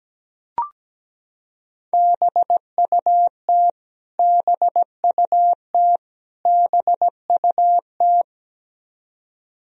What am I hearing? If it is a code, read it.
Morse code, heard as a keyed tone, about 17 wpm, 700 Hz. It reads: BUT BUT BUT
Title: Top 100 Words Repeated 17wpm